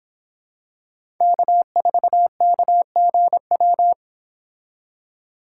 Morse code, heard as a keyed tone, about 26 wpm, 700 Hz. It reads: K4KGW